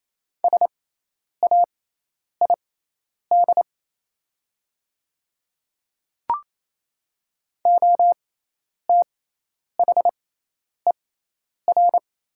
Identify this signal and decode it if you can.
Morse code, heard as a keyed tone, about 27 wpm, spaced as for 10 wpm, 690 Hz. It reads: SAID OTHER
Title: 6x Character Spacing - Top 100 Words - 28wpm